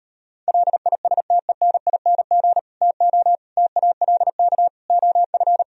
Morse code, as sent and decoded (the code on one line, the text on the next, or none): .-.. .. ... - . -. .. -. --. / - --- / - .- .-.. -.- / --- ..-.
LISTENING TO TALK OF